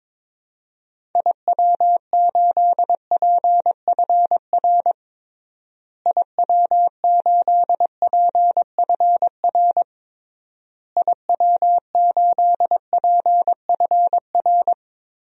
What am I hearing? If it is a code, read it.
Morse code, heard as a keyed tone, about 22 wpm, 700 Hz. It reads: IW8PFR IW8PFR IW8PFR
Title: DXCC-100 Call Signs - 6 Characters - Repeated 3x - 22wpm